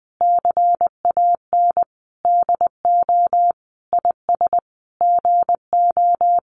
Morse code, heard as a keyed tone, about 20 wpm, 700 Hz. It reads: CAN DO IS GO
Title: Sentences from Top 200 Words 20wpm